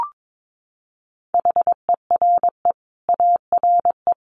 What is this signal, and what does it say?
Morse code, heard as a keyed tone, about 22 wpm, 700 Hz. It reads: HERE ARE
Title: Sets of 5 Words Encore 22wpm